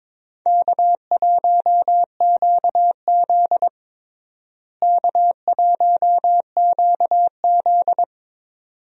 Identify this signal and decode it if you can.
Morse code, heard as a keyed tone, about 22 wpm, 700 Hz. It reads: K1QZ K1QZ
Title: USA Call Signs - 4 Characters - Repeated 2x - 22wpm